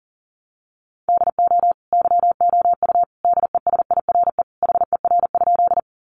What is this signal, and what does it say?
Morse code, heard as a keyed tone, about 40 wpm, 700 Hz. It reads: DO YOU DESIRE HER?
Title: Sentences from Top 1100 Words 40wpm